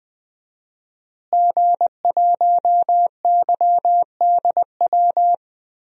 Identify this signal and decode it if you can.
Morse code, heard as a keyed tone, about 20 wpm, 700 Hz. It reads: G1YDW
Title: DXCC-100 Call Signs - 5 Characters - Repeated 2x - 20wpm